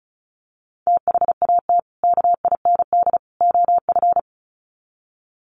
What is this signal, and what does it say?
Morse code, heard as a keyed tone, about 34 wpm, 700 Hz. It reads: THAT KIND OF
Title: Sets of 3 Words 35wpm